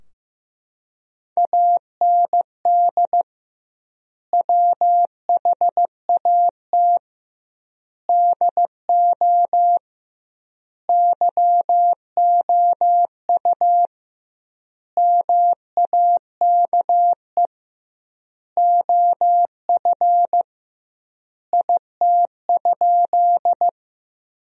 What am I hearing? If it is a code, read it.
Morse code, heard as a keyed tone, about 15 wpm, 700 Hz. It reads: AND WHAT DO YOU MAKE OF IT?
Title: 2x Word-Spacing - Sentences from Top 100 Words - 15wpm